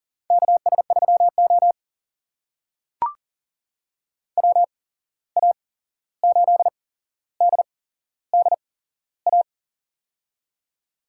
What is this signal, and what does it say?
Morse code, heard as a keyed tone, about 40 wpm, spaced as for 15 wpm, 700 Hz. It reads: KS3O WA8DDA
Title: USA Call Signs - 8x Character Spacing - 40wpm